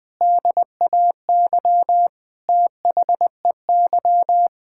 Morse code, heard as a keyed tone, about 21 wpm, 700 Hz. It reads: DAY THEY
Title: Sentences with 4-Word Phrases - 20wpm